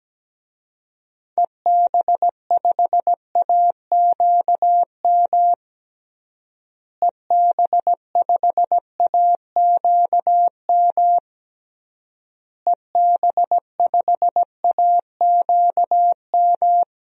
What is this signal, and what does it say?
Morse code, heard as a keyed tone, about 17 wpm, 700 Hz. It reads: EB5AQM EB5AQM EB5AQM
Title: DXCC-100 Call Signs - 6 Characters - Repeated 3x - 17wpm